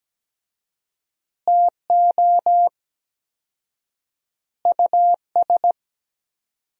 Morse code, heard as a keyed tone, about 17 wpm, 700 Hz. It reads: TO US
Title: 4x Word-Spacing - Sentences from Top 500 Words - 17wpm